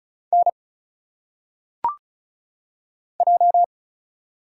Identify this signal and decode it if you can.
Morse code, heard as a keyed tone, about 34 wpm, 700 Hz. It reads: N J
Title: Single Letters - Phonetic Pronunciation 35wpm